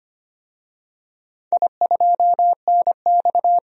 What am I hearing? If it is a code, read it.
Morse code, heard as a keyed tone, about 25 wpm, 700 Hz. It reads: I2NX
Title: DXCC-100 Call Signs - 4 Characters - 25wpm